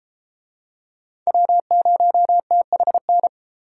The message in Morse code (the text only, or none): W0THN